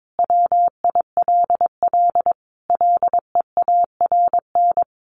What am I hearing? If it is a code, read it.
Morse code, heard as a keyed tone, about 22 wpm, 700 Hz. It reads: WILL LEARN
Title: Sentences from Top 600 Words 22wpm